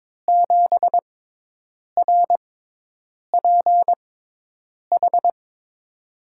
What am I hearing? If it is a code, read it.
Morse code, heard as a keyed tone, about 22 wpm, spaced as for 8 wpm, 700 Hz. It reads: 7RPH
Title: USA Call Signs - 6 Characters - 6x Character Spacing - 22wpm